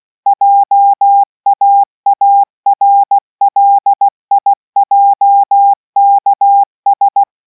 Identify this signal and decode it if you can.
Morse code, heard as a keyed tone, about 16 wpm, 800 Hz. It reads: JAARLIJKS